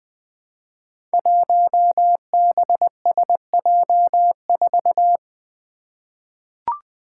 Morse code, heard as a keyed tone, about 20 wpm, 700 Hz. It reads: 1BSJ4
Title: Five Challenging Letter-Number Combinations 20wpm